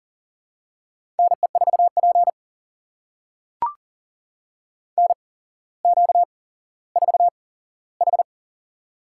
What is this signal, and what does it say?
Morse code, heard as a keyed tone, about 41 wpm, spaced as for 17 wpm, 700 Hz. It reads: NE4P NQ4H
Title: USA Call Signs - 4 Characters - 8x Character Spacing - 40wpm